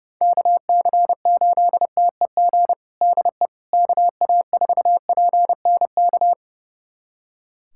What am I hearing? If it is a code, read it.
Morse code, heard as a keyed tone, about 30 wpm, 700 Hz. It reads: KC8TEG DE KA4PNK
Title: USA Callsign DE Callsign 30wpm